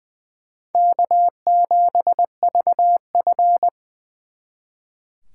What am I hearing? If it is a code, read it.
Morse code, heard as a keyed tone, about 23 wpm, 700 Hz. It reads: K7VF
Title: Speed Racing - USA 1x2 Call Signs 20wpm